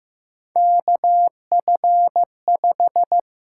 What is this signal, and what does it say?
Morse code, heard as a keyed tone, about 15 wpm, 700 Hz. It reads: KF5
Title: USA Call Signs - 6 Characters - 8x Character Spacing - 15wpm